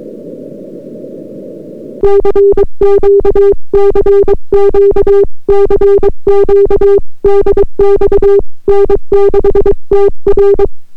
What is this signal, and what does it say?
Morse code, heard as a keyed tone, about 22 wpm, 400 Hz. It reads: CQ CQ CQ DX N6TR